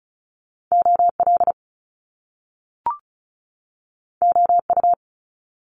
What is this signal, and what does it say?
Morse code, heard as a keyed tone, about 36 wpm, 700 Hz. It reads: OL OU
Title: Top Two Letter Combinations in Words 35wpm